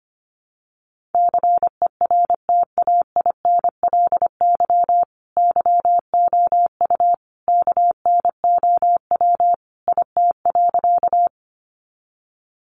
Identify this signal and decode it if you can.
Morse code, heard as a keyed tone, about 25 wpm, 700 Hz. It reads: CERTAINLY YOU KNOW IT.
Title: Sentences from Top 900 Words 25wpm